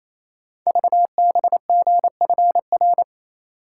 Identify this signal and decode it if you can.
Morse code, heard as a keyed tone, about 28 wpm, 700 Hz. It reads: VBGFR
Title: Code Groups - 5 Letters - 28wpm